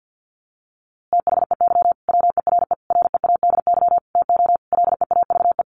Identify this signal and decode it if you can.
Morse code, heard as a keyed tone, about 50 wpm, 700 Hz. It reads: THEY WERE READY TO LEAVE
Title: A Sound Of Thunder 50wpm